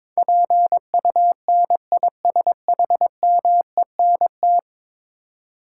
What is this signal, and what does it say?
Morse code, heard as a keyed tone, about 22 wpm, 700 Hz. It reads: PUNISHMENT
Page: morse-code-ninja-practice-22wpm